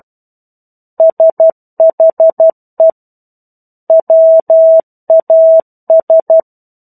Morse code, heard as a keyed tone, about 12 wpm, 650 Hz. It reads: SHE WAS
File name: 01-Chapter-01 12 WPM